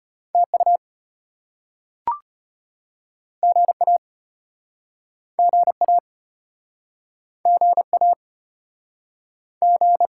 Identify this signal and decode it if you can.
Morse code, heard as a keyed tone, about 36 wpm, 700 Hz. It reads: TU GA GA GA G